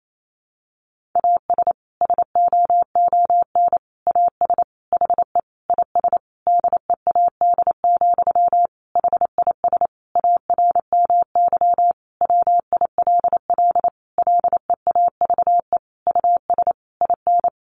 Morse code, heard as a keyed tone, about 28 wpm, 700 Hz. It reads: AS SOON AS HE IS DEAD, HIS ARMY WILL LEAVE US IN
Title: Sentences from Top 1500 Words 28wpm